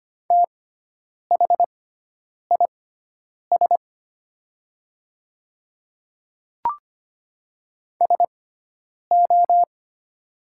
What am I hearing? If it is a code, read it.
Morse code, heard as a keyed tone, about 25 wpm, spaced as for 9 wpm, 690 Hz. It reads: THIS SO
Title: 6x Character Spacing - Top 100 Words - 25wpm